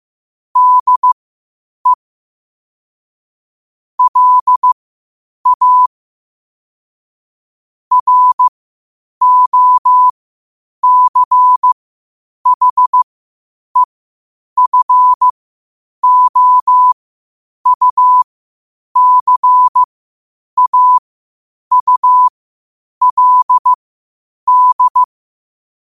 Morse code, heard as a keyed tone, about 15 wpm, spaced as for 8 wpm, 1000 Hz. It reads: DE LA ROCHEFOUCAULD
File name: qotdfiles / 8wpm